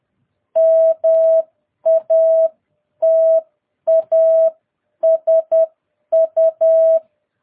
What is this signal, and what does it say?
Morse code, heard as a keyed tone, about 10 wpm, spaced as for 9 wpm, 650 Hz. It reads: MATASU
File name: jh8cbh / 2022cw